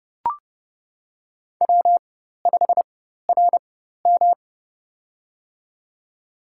Morse code, heard as a keyed tone, about 30 wpm, spaced as for 14 wpm, 700 Hz. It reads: W5RM
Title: USA Call Signs - 4 Characters - 4x Character Spacing - 30wpm